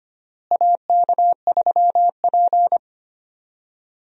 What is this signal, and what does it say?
Morse code, heard as a keyed tone, about 25 wpm, 700 Hz. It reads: AK3P
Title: USA Call Signs - 4 Characters - Repeated 2x - 25wpm